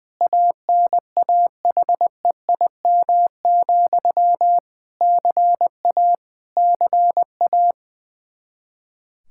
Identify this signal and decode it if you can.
Morse code, heard as a keyed tone, about 20 wpm, 700 Hz. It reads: ANAHEIM, CA CA
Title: Practice QSO - Exchange 1 - Beginning - 20wpm